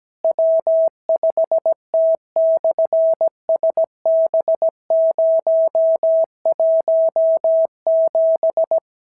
Morse code, heard as a keyed tone, about 17 wpm, 650 Hz. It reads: W5T/SB017
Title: SOTA QSO Practice 17wpm